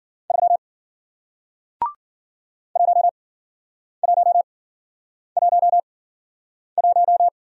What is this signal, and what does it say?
Morse code, heard as a keyed tone, about 54 wpm, spaced as for 9 wpm, 700 Hz. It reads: 3 1111